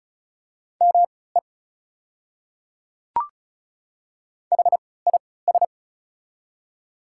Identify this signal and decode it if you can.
Morse code, heard as a keyed tone, about 34 wpm, spaced as for 20 wpm, 700 Hz. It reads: ME HIS
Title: 3x Character Spacing - Top 300 Words - 35wpm